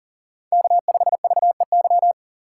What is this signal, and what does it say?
Morse code, heard as a keyed tone, about 40 wpm, 700 Hz. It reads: K5VEY
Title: USA Call Signs - 5 Characters - 6x Character Spacing - 40wpm